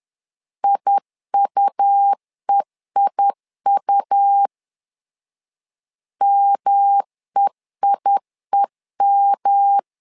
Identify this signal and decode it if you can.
Morse code, heard as a keyed tone, about 10 wpm, 800 Hz. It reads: IUEIU MEIEM
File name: LEZIONI / 3 MWRO / 3 M 10wpm 1828